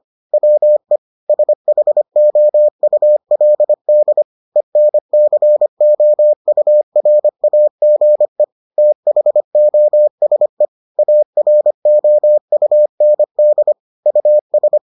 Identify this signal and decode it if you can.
Morse code, heard as a keyed tone, about 25 wpm, 600 Hz. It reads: WE SHOULD ENCOURAGE THOSE AROUND US